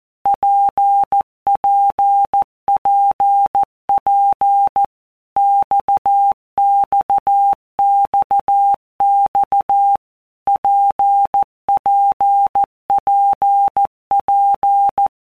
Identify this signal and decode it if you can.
Morse code, heard as a keyed tone, about 14 wpm, 800 Hz. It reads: PPPP XXXX PPPP
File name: All Groups 1-minute Blocks / Groups@13 / RKPX